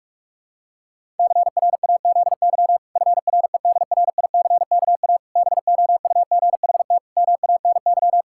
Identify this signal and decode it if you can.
Morse code, heard as a keyed tone, about 41 wpm, 700 Hz. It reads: CRAZY FREDRICKA BOUGHT MANY
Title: Pangram Challenge v2 - 45wpm